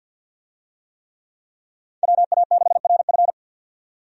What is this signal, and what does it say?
Morse code, heard as a keyed tone, about 50 wpm, 700 Hz. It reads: WA6RF
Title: USA Call Signs - 5 Characters - 50wpm